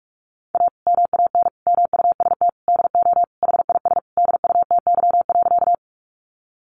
Morse code, heard as a keyed tone, about 45 wpm, 700 Hz. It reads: A MAN MUST DO HIS DUTY.
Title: Sentences from Top 1500 Words 45wpm